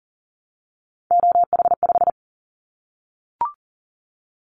Morse code, heard as a keyed tone, about 40 wpm, 700 Hz. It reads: OH5
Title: Callsign Prefixes for Top 25 DXCC - 40wpm